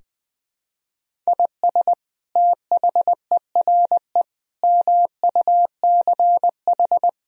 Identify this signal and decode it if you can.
Morse code, heard as a keyed tone, about 20 wpm, 700 Hz. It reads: IS THERE MUCH
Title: Sentences from Top 1000 Words - Review Part 4 - 20wpm